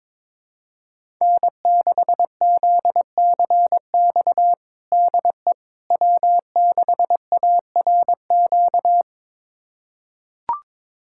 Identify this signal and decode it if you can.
Morse code, heard as a keyed tone, about 21 wpm, 700 Hz. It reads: N6ZCX DE W6ARQ
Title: USA Callsign DE Callsign - 5 Characters - 22wpm